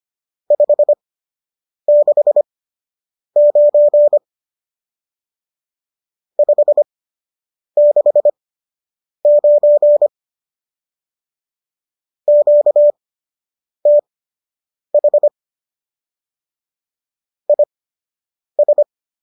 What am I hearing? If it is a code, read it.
Morse code, heard as a keyed tone, about 25 wpm, spaced as for 8 wpm, 590 Hz. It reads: 569 569 QTH IS